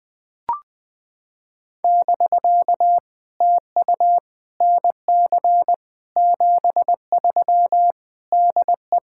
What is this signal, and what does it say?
Morse code, heard as a keyed tone, about 20 wpm, 700 Hz. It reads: <BK> TU NC 73 DE
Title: POTA QSO Practice - 20wpm